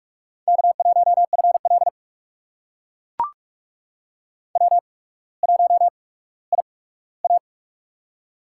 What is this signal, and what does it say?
Morse code, heard as a keyed tone, about 44 wpm, spaced as for 19 wpm, 700 Hz. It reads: K1FL W1IA
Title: USA Call Signs - 4 Characters - 8x Character Spacing - 45wpm